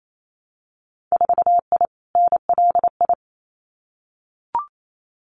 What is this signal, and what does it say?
Morse code, heard as a keyed tone, about 28 wpm, 700 Hz. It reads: 4I NLI
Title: Winter Field Day Exchanges Encore 28wpm